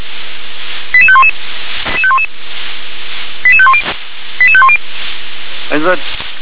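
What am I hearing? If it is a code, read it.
Five-tone selective call (ZVEI), heard as quick runs of five tones, tones of about 70 ms. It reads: 88511 88510 88510
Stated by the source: ZVEI five-tone call received by SDR.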